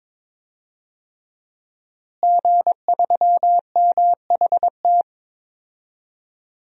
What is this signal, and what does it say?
Morse code, heard as a keyed tone, about 22 wpm, 700 Hz. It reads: G3MHT
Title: DXCC-100 Call Signs - 5 Characters - Repeated 3x - 22wpm